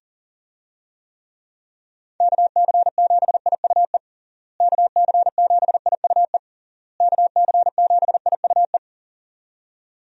Morse code, heard as a keyed tone, about 40 wpm, 700 Hz. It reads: KC7IUE KC7IUE KC7IUE
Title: USA Call Signs - 6 Characters - Repeated 3x - 40wpm